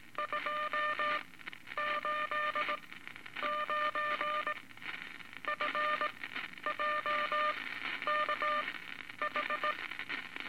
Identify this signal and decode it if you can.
Morse code, heard as a keyed tone, about 19 wpm, spaced as for 11 wpm, 1200 Hz, with noise. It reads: 2 8 9 F JK H
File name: Morse Code Records / 17A LCM 162 Eight Words Per Minute